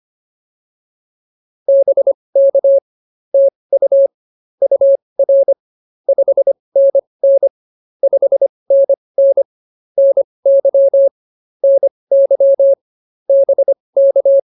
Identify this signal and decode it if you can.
Morse code, heard as a keyed tone, about 24 wpm, spaced as for 20 wpm, 550 Hz. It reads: BK TU UR 5NN 5NN NY NY BK